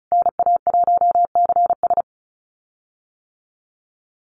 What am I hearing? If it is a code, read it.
Morse code, heard as a keyed tone, about 34 wpm, 700 Hz. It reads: NA1CS
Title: USA Call Signs Repeated 2x 35wpm